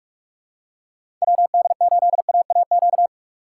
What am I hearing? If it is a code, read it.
Morse code, heard as a keyed tone, about 44 wpm, 700 Hz. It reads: WD8AAQ
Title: USA Call Signs - 6 Characters - 4x Character Spacing - 45wpm